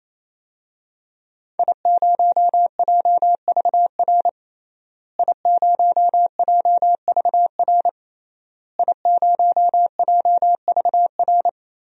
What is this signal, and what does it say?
Morse code, heard as a keyed tone, about 28 wpm, 700 Hz. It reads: I0JVR I0JVR I0JVR
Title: DXCC-100 Call Signs - 5 Characters - Repeated 3x - 28wpm